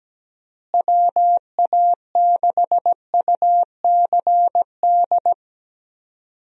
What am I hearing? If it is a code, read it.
Morse code, heard as a keyed tone, about 17 wpm, 700 Hz. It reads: WA6UCD
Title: USA Call Signs - 6 Characters - Repeated 3x - 17wpm